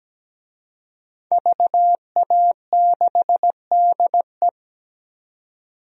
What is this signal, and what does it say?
Morse code, heard as a keyed tone, about 17 wpm, 700 Hz. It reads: VA6DE
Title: DXCC-100 Call Signs - 5 Characters - 17wpm